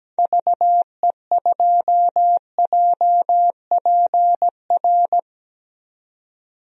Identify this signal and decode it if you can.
Morse code, heard as a keyed tone, about 17 wpm, 700 Hz. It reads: VE2JPR
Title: DXCC-100 Call Signs - 6 Characters - 17wpm